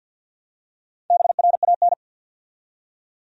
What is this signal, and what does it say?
Morse code, heard as a keyed tone, about 48 wpm, 700 Hz. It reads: BRAN
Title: USA Common Names - Max 5 Letters - 50wpm